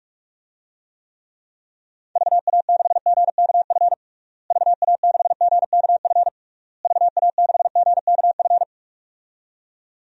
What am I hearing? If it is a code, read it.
Morse code, heard as a keyed tone, about 46 wpm, 700 Hz. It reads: VA6GKF VA6GKF VA6GKF
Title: DXCC-100 Call Signs - 6 Characters - Repeated 3x - 45wpm